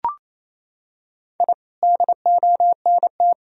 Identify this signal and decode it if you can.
Morse code, heard as a keyed tone, about 28 wpm, 700 Hz. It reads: I DONT
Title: Sentences from Top 1000 Words - Review Part 3 - 28wpm